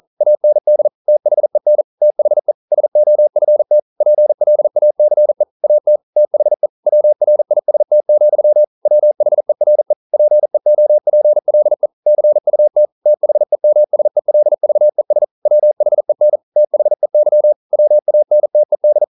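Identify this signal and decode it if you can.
Morse code, heard as a keyed tone, about 41 wpm, 600 Hz. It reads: AND THEN THE SOFT PLACE AT THE WRIST, WHERE PEOPLE CUT THEMSELVES WHEN THEY WANTED
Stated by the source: Morse code practice at 41 wpm, from CW med Gnister.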